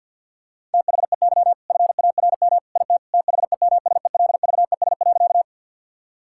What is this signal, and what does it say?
Morse code, heard as a keyed tone, about 50 wpm, 700 Hz. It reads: THEY FARM IT THEMSELVES.
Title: Sentences from Top 1000 Words - Review Part 5 - 50wpm